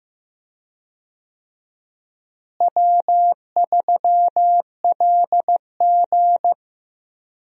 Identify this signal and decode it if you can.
Morse code, heard as a keyed tone, about 15 wpm, 700 Hz. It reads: W3LG
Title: USA Call Signs - 4 Characters - Repeated 3x - 15wpm